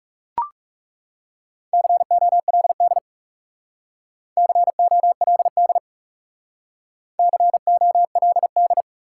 Morse code, heard as a keyed tone, about 39 wpm, 700 Hz. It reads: COLD COLD COLD